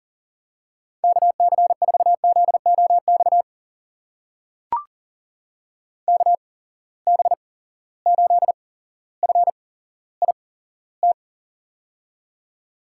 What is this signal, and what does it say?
Morse code, heard as a keyed tone, about 38 wpm, spaced as for 17 wpm, 700 Hz. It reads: KC4ZOX KB8FIT
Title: USA Call Signs - 6 Characters - 8x Character Spacing - 40wpm